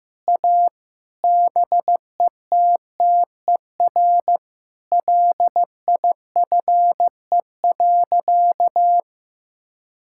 Morse code, heard as a keyed tone, about 15 wpm, 700 Hz. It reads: A BETTER LIFE.
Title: Sentences from Top 1000 Words - Review Part 5 - 15wpm